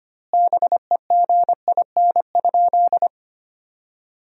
Morse code, heard as a keyed tone, about 25 wpm, 700 Hz. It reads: BEGIN?
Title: 2x Word-Spacing - Sentences from Top 300 Words - 25wpm